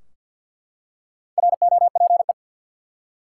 Morse code, heard as a keyed tone, about 48 wpm, 700 Hz. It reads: ROPE
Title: Four Letter Words - 50wpm